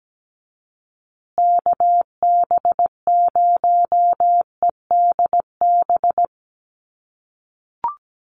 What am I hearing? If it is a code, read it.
Morse code, heard as a keyed tone, about 17 wpm, 700 Hz. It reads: KB0EDB
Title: USA Call Signs Repeated 3x 17wpm